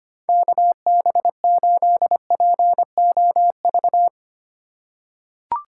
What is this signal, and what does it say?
Morse code, heard as a keyed tone, about 25 wpm, 700 Hz. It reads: KB8POV
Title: USA Call Signs - 6 Characters - 4x Character Spacing - 25wpm